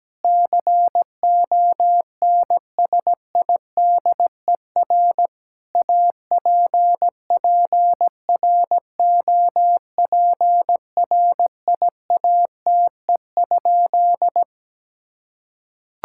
Morse code, heard as a keyed tone, about 17 wpm, 700 Hz. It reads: CONSIDER APPROPRIATE?
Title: Sentences from Top 1000 Words - Review Part 5 - 17wpm